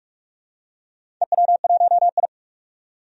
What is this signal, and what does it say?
Morse code, heard as a keyed tone, about 41 wpm, 700 Hz. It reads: EW1I
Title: DXCC-100 Call Signs - 4 Characters - 45wpm